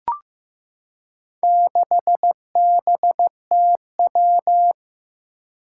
Morse code, heard as a keyed tone, about 15 wpm, 700 Hz. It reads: 6BTW
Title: Four Challenging Letter-Number Combinations 15wpm